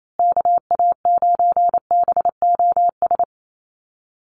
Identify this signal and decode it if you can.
Morse code, heard as a keyed tone, about 28 wpm, 690 Hz. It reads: KA9BOS
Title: USA Call Signs Repeated 3x 28wpm